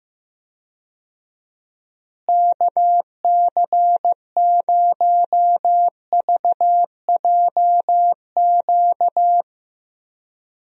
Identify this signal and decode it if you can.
Morse code, heard as a keyed tone, about 15 wpm, 700 Hz. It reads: KC0VJQ
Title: USA Call Signs - 6 Characters - Repeated 3x - 15wpm